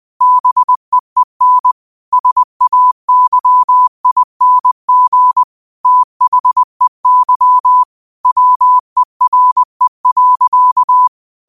Morse code, heard as a keyed tone, about 20 wpm, 1000 Hz. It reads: BEEN SAYING THEY WERE.
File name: qotdfiles / 20wpm